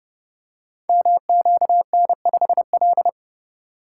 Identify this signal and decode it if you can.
Morse code, heard as a keyed tone, about 28 wpm, 700 Hz. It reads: MQN5L
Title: Code Groups - 5 Letter:Numbers - 30wpm